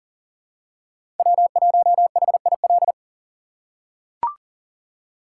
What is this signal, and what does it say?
Morse code, heard as a keyed tone, about 40 wpm, 700 Hz. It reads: W1HIL
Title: USA Call Signs - 5 Characters - 2x Character Spacing - 40wpm